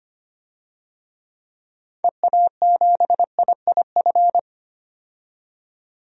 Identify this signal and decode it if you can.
Morse code, heard as a keyed tone, about 25 wpm, 700 Hz. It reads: EA7IIF
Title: DXCC-100 Call Signs - 6 Characters - 25wpm